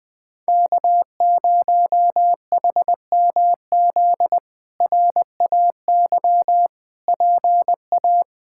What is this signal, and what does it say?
Morse code, heard as a keyed tone, about 20 wpm, 700 Hz. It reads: K0HMZ RAY PA
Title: K1USN SST QSO Practice - 20wpm